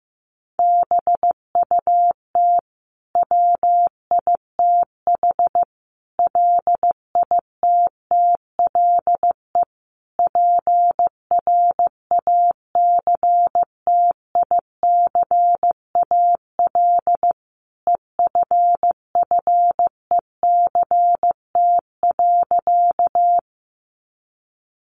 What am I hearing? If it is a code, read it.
Morse code, heard as a keyed tone, about 15 wpm, 700 Hz. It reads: BUT WITH LITTLE PRACTICAL EFFECT.
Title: Sentences from Top 1400 Words 15wpm